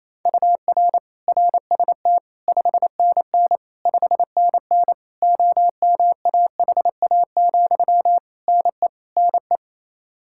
Morse code, heard as a keyed tone, about 27 wpm, 700 Hz. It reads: UR RST 5NN 5NN OMAHA, NE NE
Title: Practice QSO - Exchange 1 - Beginning - 28wpm